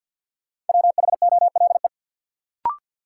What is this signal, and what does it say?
Morse code, heard as a keyed tone, about 53 wpm, 700 Hz. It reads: WHOLE
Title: Sets of 2 Words - New Word Drill - 50wpm